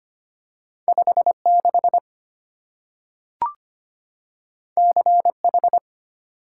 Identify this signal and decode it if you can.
Morse code, heard as a keyed tone, about 25 wpm, 700 Hz. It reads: 56 CH
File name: Two Letter-Number Combinations - 25wpm